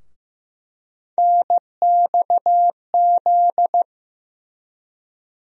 Morse code, heard as a keyed tone, about 15 wpm, 700 Hz. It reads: NXZ
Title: USA Call Sign Postfixes - 3 Letters 15wpm